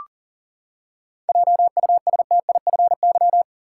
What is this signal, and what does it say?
Morse code, heard as a keyed tone, about 40 wpm, 700 Hz. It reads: JUSTIFY
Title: Seven Letter Words - 40wpm